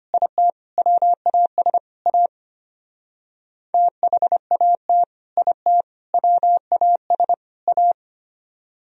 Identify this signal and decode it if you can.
Morse code, heard as a keyed tone, about 32 wpm, 700 Hz. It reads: IT WAS A THAT IT WAS A